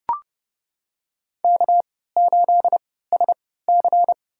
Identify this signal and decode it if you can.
Morse code, heard as a keyed tone, about 30 wpm, spaced as for 17 wpm, 700 Hz. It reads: K8SC
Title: USA Call Signs - 4 Characters - 3x Character Spacing - 30wpm